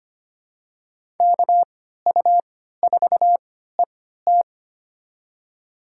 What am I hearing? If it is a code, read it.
Morse code, heard as a keyed tone, about 25 wpm, spaced as for 14 wpm, 700 Hz. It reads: KU4ET
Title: USA Call Signs - 5 Characters - 3x Character Spacing - 25wpm